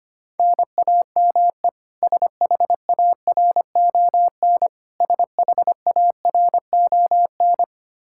Morse code, heard as a keyed tone, about 25 wpm, 700 Hz. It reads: NAME SHARON SHARON
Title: K1USN SST QSO Practice 25wpm